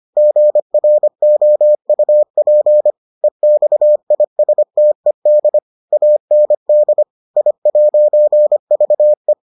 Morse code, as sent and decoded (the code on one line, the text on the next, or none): --. .-. --- ..- .--. / . -..- .. ... - . -.. / .- -. -.. / .. .----. ...- .
GROUP EXISTED AND I'VE